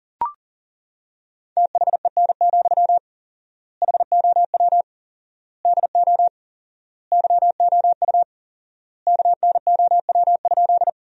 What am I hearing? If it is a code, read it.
Morse code, heard as a keyed tone, about 40 wpm, 700 Hz. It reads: THEN, HOW DO YOU KNOW?
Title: 4x Word-Spacing - Sentences from Top 100 Words - 40wpm